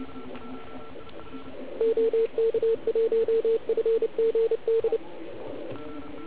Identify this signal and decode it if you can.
Morse code, heard as a keyed tone, about 28 wpm, 450 Hz. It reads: OK1FGD
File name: Archiv / CUC129 / CUC129 pileup